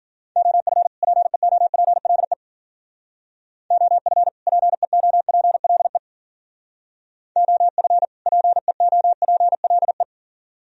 Morse code, heard as a keyed tone, about 49 wpm, 690 Hz. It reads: OF PEOPLE OF PEOPLE OF PEOPLE
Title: Speed-Racing - Sets of 2 Words 40wpm